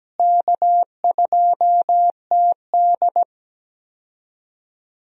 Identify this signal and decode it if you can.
Morse code, heard as a keyed tone, about 17 wpm, 700 Hz. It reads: K2TD
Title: USA Call Signs - 4 Characters - Repeated 2x - 17wpm